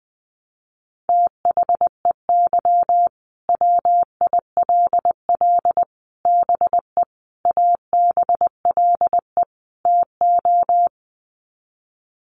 Morse code, heard as a keyed tone, about 20 wpm, 700 Hz. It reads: THEY WILL BE ABLE TO
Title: Sets of 5 Words Encore 20wpm